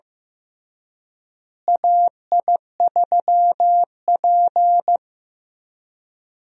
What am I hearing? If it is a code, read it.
Morse code, heard as a keyed tone, about 15 wpm, 700 Hz. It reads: AI3P
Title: USA Call Signs - 4 Characters - Repeated 3x - 15wpm